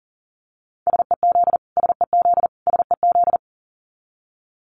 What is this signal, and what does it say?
Morse code, heard as a keyed tone, about 40 wpm, 700 Hz. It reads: SEZ SEZ SEZ
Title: Abbreviations Repeated 3x - 40wpm